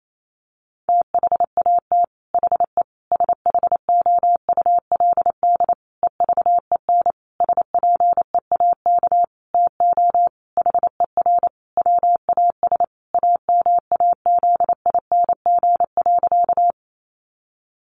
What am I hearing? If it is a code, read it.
Morse code, heard as a keyed tone, about 28 wpm, 700 Hz. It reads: THAT HE SHOULD EVEN SPEAK TO HER WAS AMAZING.
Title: Sentences from Top 1500 Words 28wpm